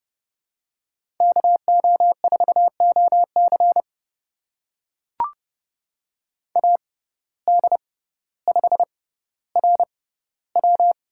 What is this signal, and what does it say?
Morse code, heard as a keyed tone, about 30 wpm, spaced as for 15 wpm, 700 Hz. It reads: KO4OC AD5RW
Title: USA Call Signs - 6x Character Spacing - 30wpm